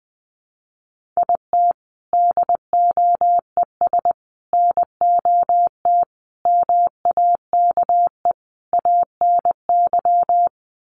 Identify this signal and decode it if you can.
Morse code, heard as a keyed tone, about 20 wpm, 700 Hz. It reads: IT DOES NOT MAKE ANY
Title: Sets of 5 Words Encore2 20wpm